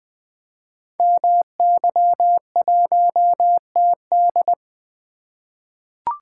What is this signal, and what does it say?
Morse code, heard as a keyed tone, about 20 wpm, 700 Hz. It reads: MY1TD
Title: Code Groups - 5 Letter:Numbers - 20wpm